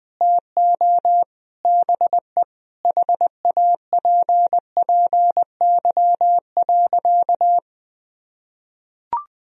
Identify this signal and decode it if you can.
Morse code, heard as a keyed tone, about 20 wpm, 700 Hz. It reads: TO BE HAPPY.
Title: Sentences from Top 1000 Words - Review Part 1 - 20wpm